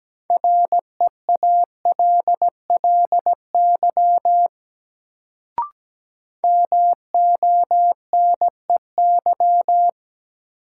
Morse code, heard as a keyed tone, about 17 wpm, 700 Hz. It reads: REALLY MONEY
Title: Sets of 4 Words - New Word Drill - 17wpm